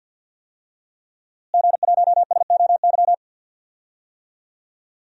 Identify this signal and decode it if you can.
Morse code, heard as a keyed tone, about 51 wpm, 700 Hz. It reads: G1SOY
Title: DXCC-100 Call Signs - 5 Characters - Repeated 2x - 50wpm